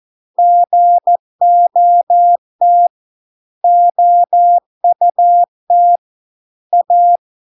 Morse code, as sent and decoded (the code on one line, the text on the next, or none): --. --- - / --- ..- - / .-
GOT OUT A